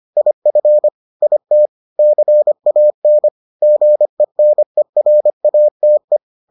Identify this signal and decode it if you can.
Morse code, heard as a keyed tone, about 25 wpm, 600 Hz. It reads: IF IT CAN GENERATE